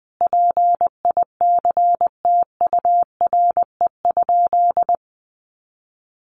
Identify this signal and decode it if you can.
Morse code, heard as a keyed tone, about 20 wpm, 700 Hz. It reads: PICTURE?
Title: A Sound Of Thunder 20wpm